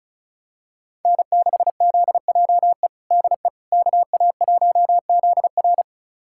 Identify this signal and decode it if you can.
Morse code, heard as a keyed tone, about 34 wpm, 690 Hz. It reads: N6ZJE DE KA1ZR
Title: USA Callsign DE Callsign - 5 Characters - 35wpm